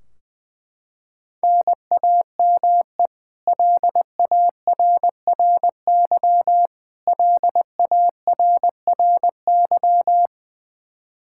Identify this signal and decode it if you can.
Morse code, heard as a keyed tone, about 20 wpm, 700 Hz. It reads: NAME LARRY LARRY
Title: Practice QSO - Exchange 1 - Beginning - 20wpm